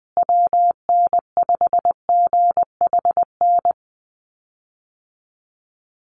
Morse code, heard as a keyed tone, about 20 wpm, 700 Hz. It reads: WN5GHN
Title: USA Call Signs Repeated 2x 20wpm